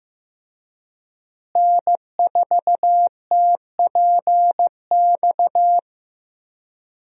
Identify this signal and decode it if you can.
Morse code, heard as a keyed tone, about 15 wpm, 700 Hz. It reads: N4TPX
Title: USA Call Signs - 5 Characters - Repeated 3x - 15wpm